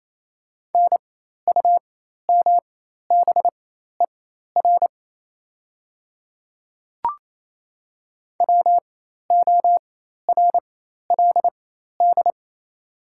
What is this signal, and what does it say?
Morse code, heard as a keyed tone, about 27 wpm, spaced as for 13 wpm, 690 Hz. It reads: NUMBER WORLD
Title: 4x Character Spacing - Top 200 Words - 28wpm